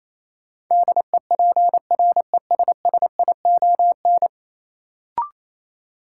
Morse code, heard as a keyed tone, about 28 wpm, 700 Hz. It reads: DEPRESSION
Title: Sentences from Top 1300 Words - New Word Drill - 28wpm